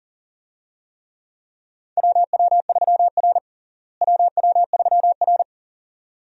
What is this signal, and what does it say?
Morse code, heard as a keyed tone, about 40 wpm, 700 Hz. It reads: WW3R WW3R
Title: USA Call Signs - 4 Characters - Repeated 2x - 40wpm